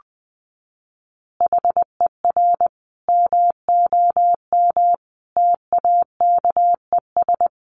Morse code, heard as a keyed tone, about 20 wpm, 700 Hz. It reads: HER MOM TAKES
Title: Additional Easy Phrases 20wpm